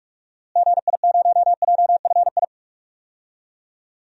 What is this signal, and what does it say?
Morse code, heard as a keyed tone, about 46 wpm, 700 Hz. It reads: GI0JFI
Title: DXCC-100 Call Signs - 6 Characters - Repeated 3x - 45wpm